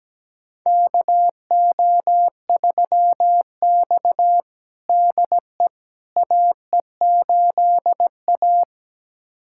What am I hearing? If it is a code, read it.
Morse code, heard as a keyed tone, about 17 wpm, 700 Hz. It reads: KO3X DE AE8A
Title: USA Callsign DE Callsign - 4 Characters - 17wpm